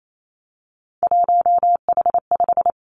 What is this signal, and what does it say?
Morse code, heard as a keyed tone, about 28 wpm, 700 Hz. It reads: 1H5
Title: Three Challenging Letter-Number Combinations 28wpm